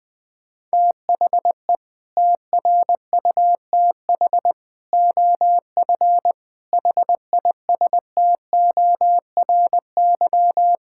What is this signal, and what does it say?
Morse code, heard as a keyed tone, about 20 wpm, 700 Hz. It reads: THE TRUTH OF HISTORY
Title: Sentences from Top 1600 Words 20wpm